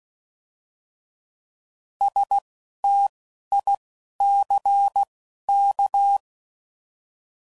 Morse code, heard as a keyed tone, about 16 wpm, 800 Hz. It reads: S T I C K